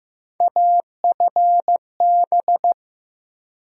Morse code, heard as a keyed tone, about 15 wpm, 700 Hz. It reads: AFB
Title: USA Call Sign Postfixes - 3 Letters 15wpm